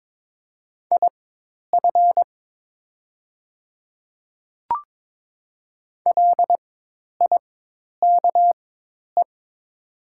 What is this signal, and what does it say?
Morse code, heard as a keyed tone, about 22 wpm, spaced as for 10 wpm, 700 Hz. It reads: IF LIKE
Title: 4x Character Spacing - Top 100 Words - 22wpm